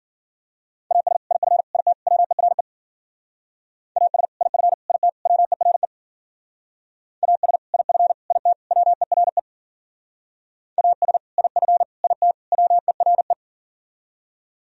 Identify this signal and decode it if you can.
Morse code, heard as a keyed tone, about 49 wpm, 700 Hz. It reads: AS IF IT WERE AS IF IT WERE AS IF IT WERE AS IF IT WERE
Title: Speed-Racing - Sets of 4 Words 40wpm